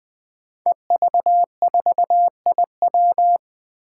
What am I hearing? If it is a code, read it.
Morse code, heard as a keyed tone, about 21 wpm, 700 Hz. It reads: EV4IW
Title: Five Challenging Letter-Number Combinations 20wpm